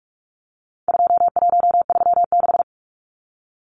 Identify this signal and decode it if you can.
Morse code, heard as a keyed tone, about 44 wpm, 700 Hz. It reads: 2136
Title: Numbers - 4-Digit 45wpm